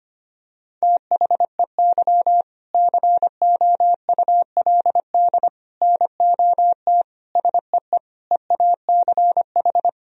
Morse code, heard as a keyed tone, about 25 wpm, 700 Hz. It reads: THEY COULD NOT SEE EACH
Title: Sentences from Top 1000 Words - Review Part 1 - 25wpm